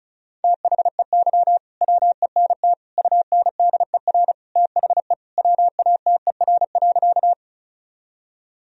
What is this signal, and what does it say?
Morse code, heard as a keyed tone, about 36 wpm, 700 Hz. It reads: THEY WENT UNDER THE WATER.
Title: Sentences from Top 1000 Words - Review Part 1 - 35wpm